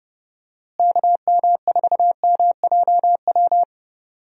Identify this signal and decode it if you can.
Morse code, heard as a keyed tone, about 31 wpm, 700 Hz. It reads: KM4MJW
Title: USA Call Signs - 6 Characters - Repeated 3x - 30wpm